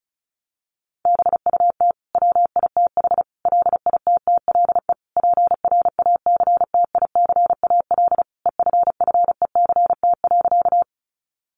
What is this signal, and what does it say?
Morse code, heard as a keyed tone, about 34 wpm, 700 Hz. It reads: BUT WITH LITTLE PRACTICAL EFFECT.
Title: Sentences from Top 1400 Words 35wpm